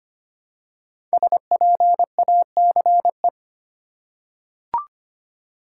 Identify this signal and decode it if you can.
Morse code, heard as a keyed tone, about 25 wpm, 700 Hz. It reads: SPACE
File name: 5 Letter Words - First 3 Letters Given - 25wpm